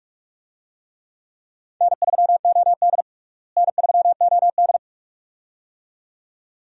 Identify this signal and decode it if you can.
Morse code, heard as a keyed tone, about 47 wpm, 700 Hz. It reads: N3OD N3OD
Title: USA Call Signs - 4 Characters - Repeated 2x - 45wpm